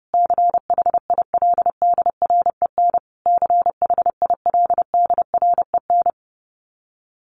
Morse code, heard as a keyed tone, about 30 wpm, 700 Hz. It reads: CHILDREN CHILDREN
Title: Top 500 Words Repeated 30wpm